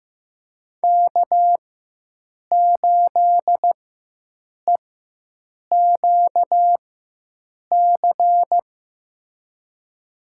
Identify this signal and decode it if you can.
Morse code, heard as a keyed tone, about 15 wpm, spaced as for 7 wpm, 700 Hz. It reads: K8EQC
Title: USA Call Signs - 5 Characters - 4x Character Spacing - 15wpm